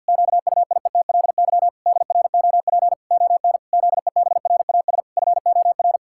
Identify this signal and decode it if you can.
Morse code, heard as a keyed tone, about 50 wpm, 700 Hz. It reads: QUIETLY DROP ON ZEBRAS FOR
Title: Pangram Challenge v2 - 50wpm